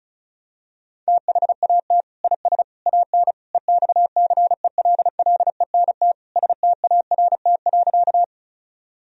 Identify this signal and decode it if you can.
Morse code, heard as a keyed tone, about 36 wpm, 700 Hz. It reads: THAT IS AN EXCELLENT START.
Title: Sentences from Top 1000 Words - Review Part 1 - 35wpm